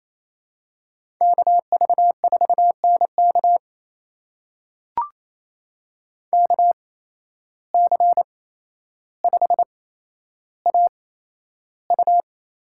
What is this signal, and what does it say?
Morse code, heard as a keyed tone, about 28 wpm, spaced as for 12 wpm, 700 Hz. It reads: KV4NK KC5AU
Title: USA Call Signs - 5 Characters - 8x Character Spacing - 28wpm